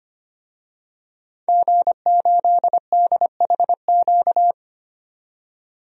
Morse code, heard as a keyed tone, about 25 wpm, 700 Hz. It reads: G8DHQ
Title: DXCC-100 Call Signs - 5 Characters - 25wpm